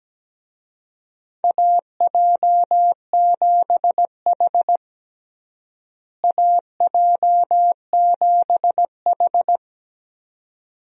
Morse code, heard as a keyed tone, about 17 wpm, 700 Hz. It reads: AJ7H AJ7H
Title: USA Call Signs - 4 Characters - Repeated 2x - 17wpm